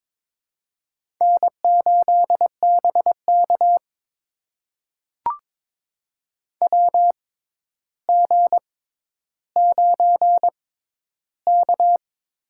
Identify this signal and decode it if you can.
Morse code, heard as a keyed tone, about 22 wpm, spaced as for 11 wpm, 700 Hz. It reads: N8BK WG9K
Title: USA Call Signs - 4 Characters - 6x Character Spacing - 22wpm